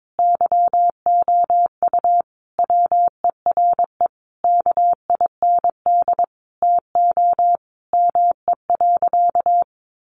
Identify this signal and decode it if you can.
Morse code, heard as a keyed tone, about 22 wpm, 700 Hz. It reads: YOU WERE KIND TO ME.
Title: Sentences from Top 1100 Words 22wpm